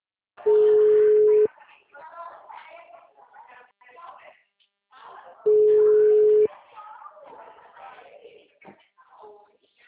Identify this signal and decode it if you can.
Ringback tone (Europe, 425 Hz), with speech.
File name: out- -2004- - - .59800